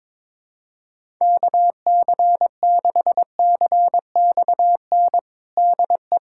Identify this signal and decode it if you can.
Morse code, heard as a keyed tone, about 22 wpm, 700 Hz. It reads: KC6CXN DE
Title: USA Callsign DE Callsign - 6 Characters - 22wpm